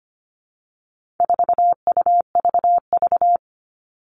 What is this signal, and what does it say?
Morse code, heard as a keyed tone, about 25 wpm, 700 Hz. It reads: 4UVV
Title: All Cut Numbers - 4-Digit 25wpm